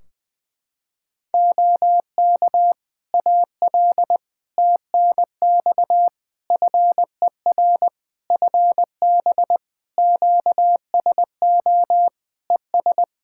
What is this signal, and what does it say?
Morse code, heard as a keyed tone, about 21 wpm, 700 Hz. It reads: OK AL TNX FER FB QSO ES
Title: Practice QSO - Exchange 4 - Ending - 20wpm